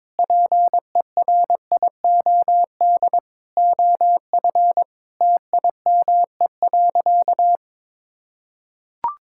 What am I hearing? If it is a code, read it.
Morse code, heard as a keyed tone, about 22 wpm, 700 Hz. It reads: PERIOD OF TIME.
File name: Sentences with 7-Word Phrases - 22wpm